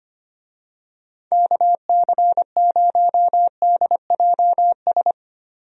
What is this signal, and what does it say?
Morse code, heard as a keyed tone, about 25 wpm, 700 Hz. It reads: KC0DJS
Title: USA Call Signs - 6 Characters - 25wpm